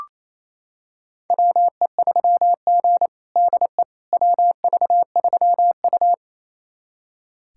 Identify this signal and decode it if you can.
Morse code, heard as a keyed tone, about 28 wpm, 700 Hz. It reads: WE3G DE WV3U
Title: USA Callsign DE Callsign - 4 Characters - 28wpm